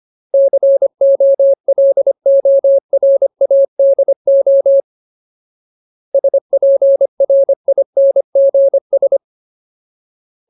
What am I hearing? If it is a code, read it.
Morse code, heard as a keyed tone, about 25 wpm, 550 Hz. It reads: COLORADO SPRINGS
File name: Ultimate Word List / 25WPM Ultimate